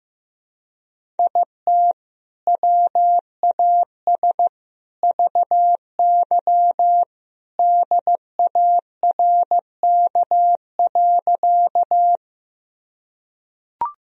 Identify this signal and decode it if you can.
Morse code, heard as a keyed tone, about 16 wpm, 700 Hz. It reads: IT WAS VY DARK.
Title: Sentences from Top 500 Words and Abbreviations 15wpm